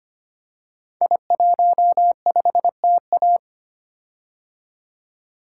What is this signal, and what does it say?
Morse code, heard as a keyed tone, about 25 wpm, 690 Hz. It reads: I15TA
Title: Code Groups - 5 Letter:Numbers - 25wpm